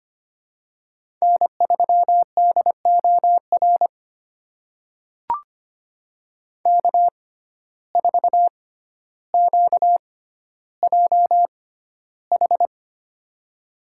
Morse code, heard as a keyed tone, about 25 wpm, spaced as for 13 wpm, 700 Hz. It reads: N3DOR K4QJH
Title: USA Call Signs - 5 Characters - 6x Character Spacing - 25wpm